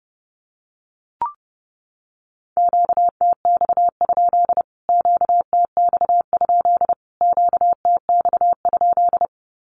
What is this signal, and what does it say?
Morse code, heard as a keyed tone, about 30 wpm, 700 Hz. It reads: QTX? QTX? QTX?
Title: Q-Codes Repeated 3x 30wpm